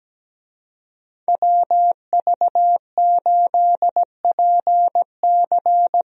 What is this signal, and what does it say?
Morse code, heard as a keyed tone, about 17 wpm, 700 Hz. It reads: WV8PC
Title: USA Call Signs - 5 Characters - Repeated 3x - 17wpm